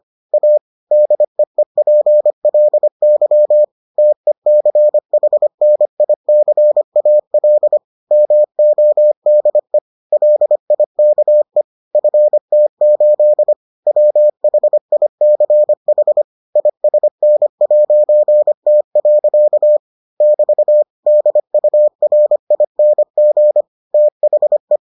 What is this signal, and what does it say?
Morse code, heard as a keyed tone, about 25 wpm, 600 Hz. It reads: A DEEPLY TECHNICAL MODE LIKE FT8 WHICH ISN'T. <BT> DURING THE